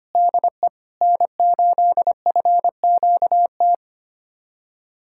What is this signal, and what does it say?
Morse code, heard as a keyed tone, about 25 wpm, 700 Hz. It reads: DE N8FQT
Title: USA Callsign DE Callsign - 5 Characters - 25wpm